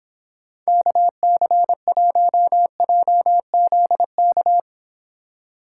Morse code, heard as a keyed tone, about 29 wpm, 700 Hz. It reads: KC1JZK